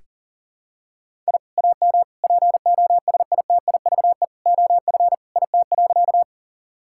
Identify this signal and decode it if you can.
Morse code, heard as a keyed tone, about 40 wpm, 700 Hz. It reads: I AM POSITIVE OF IT.
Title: Sentences from Top 1000 Words - Review Part 5 - 40wpm